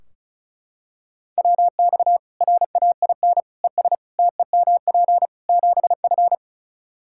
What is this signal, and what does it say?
Morse code, heard as a keyed tone, about 34 wpm, 700 Hz. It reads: WX RAIN ES TEMP 7F
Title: Practice QSO - Exchange 2 - Info - 35wpm